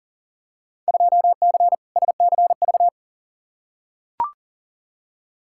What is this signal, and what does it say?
Morse code, heard as a keyed tone, about 40 wpm, 700 Hz. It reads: 2C SCV
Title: Field Day Exchanges Encore - 40wpm